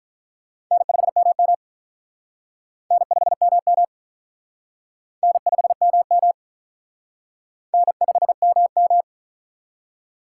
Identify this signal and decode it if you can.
Morse code, heard as a keyed tone, about 45 wpm, 700 Hz. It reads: N5MM N5MM N5MM N5MM